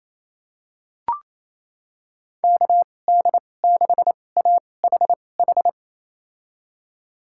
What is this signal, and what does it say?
Morse code, heard as a keyed tone, about 28 wpm, spaced as for 20 wpm, 700 Hz. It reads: KD6AHH
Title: USA Call Signs - 2x Character Spacing - 28wpm